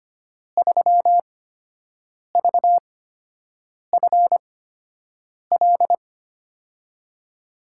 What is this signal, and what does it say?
Morse code, heard as a keyed tone, about 25 wpm, spaced as for 7 wpm, 700 Hz. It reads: 3VFL
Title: USA Call Signs - 5 Characters - 8x Character Spacing - 25wpm